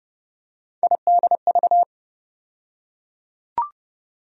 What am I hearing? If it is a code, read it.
Morse code, heard as a keyed tone, about 30 wpm, 700 Hz. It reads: IDV
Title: USA Call Sign Postfixes - 3 Letters 30wpm